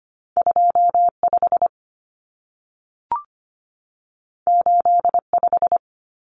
Morse code, heard as a keyed tone, about 25 wpm, 700 Hz. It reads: 25 85
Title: Numbers - 2-Digit 25wpm